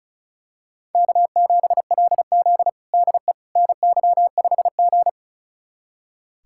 Morse code, heard as a keyed tone, about 36 wpm, 700 Hz. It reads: K7LZ DE NY5G
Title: USA Callsign DE Callsign - 4 Characters - 35wpm